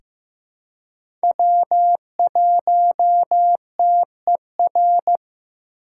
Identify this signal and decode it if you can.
Morse code, heard as a keyed tone, about 15 wpm, 700 Hz. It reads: W1TER
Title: K1USN SST QSO Practice - 15wpm